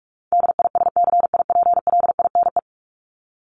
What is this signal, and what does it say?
Morse code, heard as a keyed tone, about 44 wpm, 700 Hz. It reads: DISCIPLINE
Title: Top 751-1000 Words 45wpm